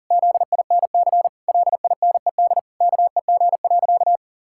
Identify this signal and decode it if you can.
Morse code, heard as a keyed tone, about 40 wpm, 700 Hz. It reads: ZINC LINED KEG.
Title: Pangram Challenge v2 - 40wpm